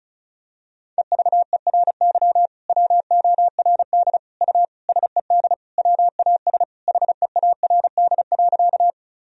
Morse code, heard as a keyed tone, about 34 wpm, 700 Hz. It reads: EVERY WORD U SED WAS HEARD.
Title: Sentences from Top 400 Words and Abbreviations 35wpm